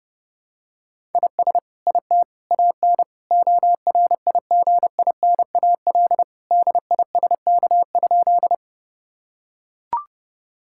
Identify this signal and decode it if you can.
Morse code, heard as a keyed tone, about 30 wpm, 700 Hz. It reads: IS IT AN ORIGINAL DISK?
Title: Sentences from Top 1000 Words - Review Part 6 - 30wpm